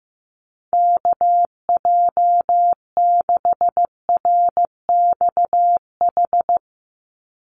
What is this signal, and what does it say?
Morse code, heard as a keyed tone, about 15 wpm, 700 Hz. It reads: KJ6RXH
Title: USA Call Signs 15wpm